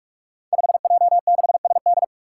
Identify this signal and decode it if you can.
Morse code, heard as a keyed tone, about 45 wpm, 700 Hz. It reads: 5J6SD
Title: Five Challenging Letter-Number Combinations 45wpm